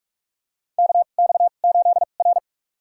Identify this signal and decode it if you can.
Morse code, heard as a keyed tone, about 43 wpm, spaced as for 32 wpm, 700 Hz. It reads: KX8R
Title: USA Call Signs - 4 Characters - 2x Character Spacing - 45wpm